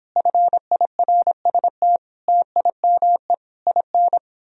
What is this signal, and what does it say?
Morse code, heard as a keyed tone, about 29 wpm, 700 Hz. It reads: FIRST TIME IN